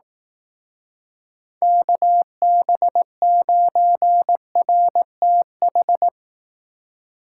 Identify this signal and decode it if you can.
Morse code, heard as a keyed tone, about 18 wpm, 700 Hz. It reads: KB9RTH
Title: Speed Racing - USA 2x3 Call Signs 15wpm